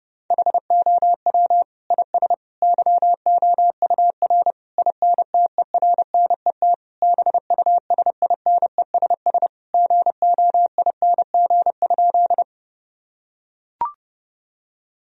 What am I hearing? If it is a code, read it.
Morse code, heard as a keyed tone, about 30 wpm, 700 Hz. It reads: HOW IS YOUR INTERNET BUSINESS GOING?
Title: Sentences from Top 1000 Words - Review Part 4 - 30wpm